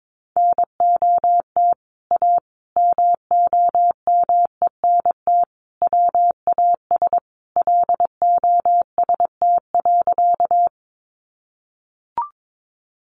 Morse code, heard as a keyed tone, about 22 wpm, 700 Hz. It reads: NOT A MOMENT WAS LOST.
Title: Sentences from Top 1400 Words 22wpm